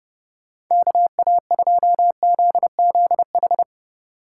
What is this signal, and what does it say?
Morse code, heard as a keyed tone, about 32 wpm, 700 Hz. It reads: KA2ZZH